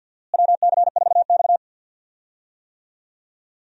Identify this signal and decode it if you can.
Morse code, heard as a keyed tone, about 50 wpm, 700 Hz. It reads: WC4X
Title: USA Call Signs - 4 Characters - Repeated 2x - 50wpm